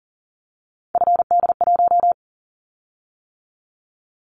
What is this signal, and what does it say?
Morse code, heard as a keyed tone, about 43 wpm, 700 Hz. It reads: FD1
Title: Callsign Prefixes for Top 50 DXCC - 40wpm